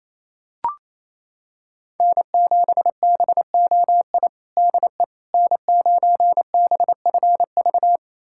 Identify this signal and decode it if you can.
Morse code, heard as a keyed tone, about 28 wpm, 700 Hz. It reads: N7BOI DE N9BFV
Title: USA Callsign DE Callsign - 5 Characters - 28wpm